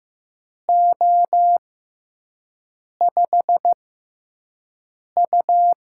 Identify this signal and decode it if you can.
Morse code, heard as a keyed tone, about 15 wpm, spaced as for 5 wpm, 700 Hz. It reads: O5U
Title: USA Call Signs - 4 Characters - 6x Character Spacing - 15wpm